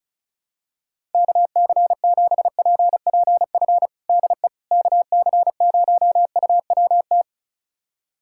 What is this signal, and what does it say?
Morse code, heard as a keyed tone, about 34 wpm, 700 Hz. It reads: KC7PPF DE KC0UWT
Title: USA Callsign DE Callsign - 6 Characters - 35wpm